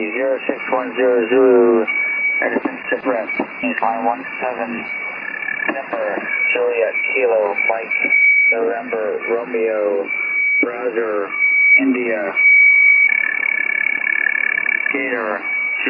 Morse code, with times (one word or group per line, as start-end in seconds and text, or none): none